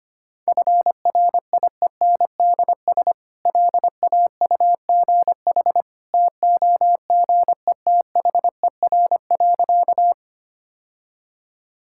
Morse code, heard as a keyed tone, about 25 wpm, 690 Hz. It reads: FRIENDS LAUGH TOGETHER.
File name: Easy Sentences with Hints - 25wpm